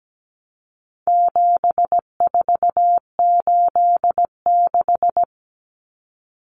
Morse code, heard as a keyed tone, about 17 wpm, 700 Hz. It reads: 7486
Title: Common Cut Numbers - 4-Digit 17wpm